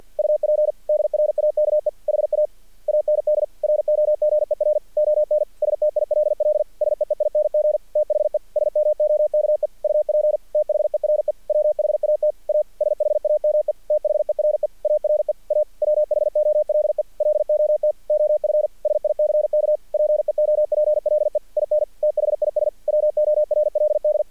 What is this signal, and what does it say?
Morse code, heard as a keyed tone, about 48 wpm, 600 Hz. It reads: UP DRAKE HA AND ROGER ON STILL SEEING THE SMOKE UP THERE WHAT A SHAME THERE ARE A WHOLE LOT OF SICK PEOPLE IN THIS WORLD
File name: archived / CRSnet